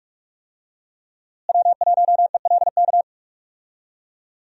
Morse code, heard as a keyed tone, about 42 wpm, 700 Hz. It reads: W1ELK
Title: USA Call Signs - 5 Characters - Repeated 3x - 45wpm